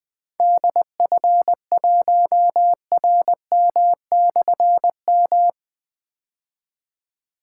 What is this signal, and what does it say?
Morse code, heard as a keyed tone, about 20 wpm, 700 Hz. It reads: DF1RM/M
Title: Difficult Call Signs 20wpm